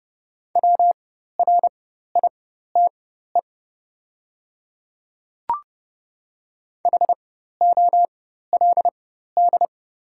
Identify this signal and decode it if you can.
Morse code, heard as a keyed tone, about 29 wpm, spaced as for 14 wpm, 700 Hz. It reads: WRITE HOLD
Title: 4x Character Spacing - Top 500 Words - 30wpm